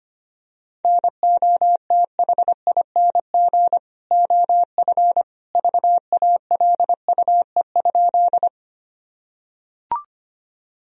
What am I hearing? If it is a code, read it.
Morse code, heard as a keyed tone, about 25 wpm, 700 Hz. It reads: NOTHING OF VALUE?
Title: Sentences from Top 1000 Words - Review Part 3 - 25wpm